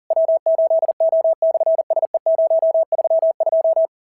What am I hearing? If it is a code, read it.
Morse code, heard as a keyed tone, about 40 wpm, 660 Hz. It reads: W8O/SE032
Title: SOTA QSO Practice 40wpm